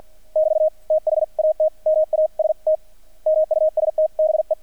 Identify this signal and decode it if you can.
Morse code, heard as a keyed tone, about 48 wpm, 630 Hz. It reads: <BT> THAT MAST MUST BE
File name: archived / CRSnet